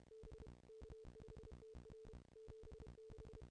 Morse code, heard as a keyed tone, about 28 wpm, 440 Hz, with noise. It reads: B K5TR 76
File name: contests / ss-cw-k5tr-op-n5rz -0400